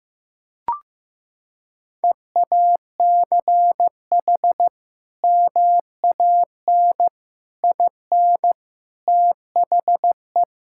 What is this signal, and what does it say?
Morse code, heard as a keyed tone, about 15 wpm, 700 Hz. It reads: EACH MAN IN THE
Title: Sentences from Top 1600 Words 15wpm